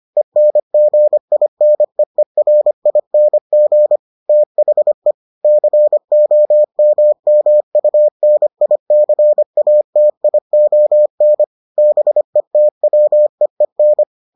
Morse code, heard as a keyed tone, about 25 wpm, 600 Hz. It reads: ENGINEERING THE COMMUNICATION BETWEEN